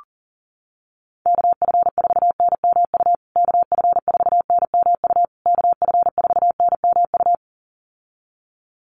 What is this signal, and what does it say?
Morse code, heard as a keyed tone, about 40 wpm, 700 Hz. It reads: KF4NMU KF4NMU KF4NMU
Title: USA Call Signs Repeated 3x 40wpm